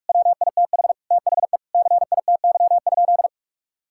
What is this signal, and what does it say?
Morse code, heard as a keyed tone, about 44 wpm, 700 Hz. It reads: WITH THE CITY?
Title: Sentences from Top 1600 Words 45wpm